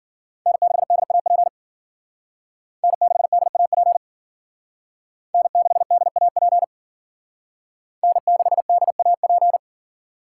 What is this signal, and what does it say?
Morse code, heard as a keyed tone, about 52 wpm, 700 Hz. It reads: N6DAP N6DAP N6DAP N6DAP